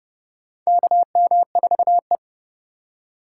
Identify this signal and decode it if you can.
Morse code, heard as a keyed tone, about 32 wpm, 700 Hz. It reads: KM4E
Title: USA Call Signs - 3x Character Spacing - 30wpm